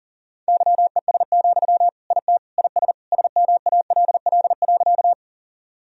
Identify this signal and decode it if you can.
Morse code, heard as a keyed tone, about 40 wpm, 700 Hz. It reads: YES, IT IS SMALL.
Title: Sentences from Top 1000 Words - Review Part 1 - 40wpm